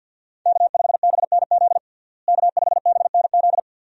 Morse code, heard as a keyed tone, about 49 wpm, 700 Hz. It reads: K5BNZ K5BNZ
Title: USA Call Signs - 5 Characters - Repeated 2x - 50wpm